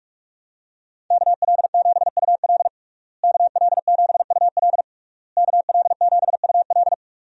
Morse code, heard as a keyed tone, about 46 wpm, 700 Hz. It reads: KL7UL KL7UL KL7UL
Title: USA Call Signs - 5 Characters - Repeated 3x - 45wpm